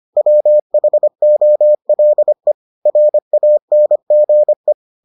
Morse code, heard as a keyed tone, about 25 wpm, 600 Hz. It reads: WHOLE RANGE